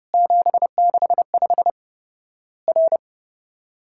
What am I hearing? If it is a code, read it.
Morse code, heard as a keyed tone, about 30 wpm, 700 Hz. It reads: 765 R
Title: Worked All Europe DX Contest - QTC Exchange - 30wpm